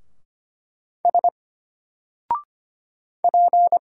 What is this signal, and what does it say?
Morse code, heard as a keyed tone, about 25 wpm, 700 Hz. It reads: S P
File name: Single-Letter-Number-25wpm